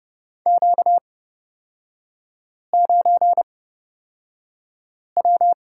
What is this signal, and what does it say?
Morse code, heard as a keyed tone, about 31 wpm, spaced as for 5 wpm, 700 Hz. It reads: Q9W
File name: Lesson 38e - Letter q - Three Characters 30wpm